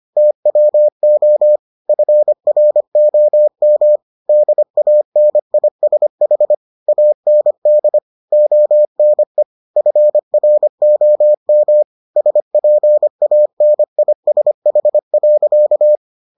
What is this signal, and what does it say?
Morse code, heard as a keyed tone, about 25 wpm, 600 Hz. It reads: TWO FROM DANISH AND ONE FROM SPANISH.